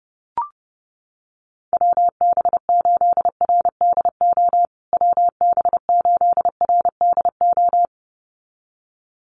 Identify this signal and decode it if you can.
Morse code, heard as a keyed tone, about 30 wpm, 700 Hz. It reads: WB8RDO WB8RDO
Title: USA Call Signs Repeated 2x 30wpm